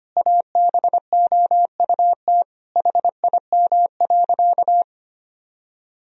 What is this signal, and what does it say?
Morse code, heard as a keyed tone, about 25 wpm, 700 Hz. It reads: ABOUT HIM.
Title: Sentences from Top 1000 Words - Review Part 1 - 25wpm